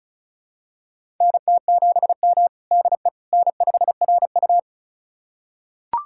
Morse code, heard as a keyed tone, about 34 wpm, 700 Hz. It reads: NT7M DE N5RU
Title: USA Callsign DE Callsign - 4 Characters - 35wpm